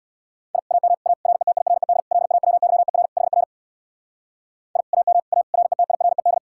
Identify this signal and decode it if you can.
Morse code, heard as a keyed tone, about 62 wpm, 700 Hz. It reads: I AM A RETIRED LAWYER HR I AM A RETIRED
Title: Speed-Racing Common QSO Phrases - 50wpm